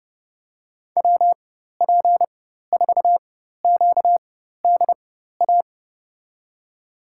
Morse code, heard as a keyed tone, about 30 wpm, spaced as for 14 wpm, 700 Hz. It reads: WP4QDA
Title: USA Call Signs - 6 Characters - 4x Character Spacing - 30wpm